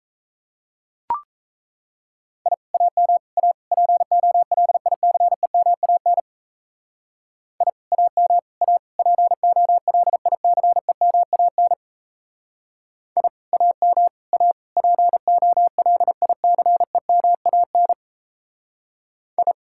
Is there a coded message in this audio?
Yes: Morse code from 2.5 to 19.5 s, heard as a keyed tone, about 37 wpm, 700 Hz. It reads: I AM A POLICEMAN I AM A POLICEMAN I AM A POLICEMAN I